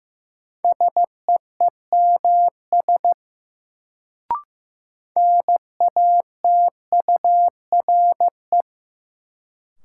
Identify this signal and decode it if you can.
Morse code, heard as a keyed tone, about 15 wpm, 700 Hz. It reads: SEEMS NATURE
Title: Sets of 3 Words - New Word Drill - 15wpm